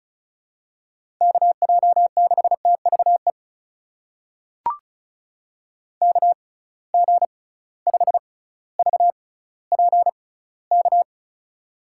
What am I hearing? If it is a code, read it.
Morse code, heard as a keyed tone, about 36 wpm, spaced as for 18 wpm, 690 Hz. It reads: KJ6TVE KG5VPK
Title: USA Call Signs - 6 Characters - 6x Character Spacing - 35wpm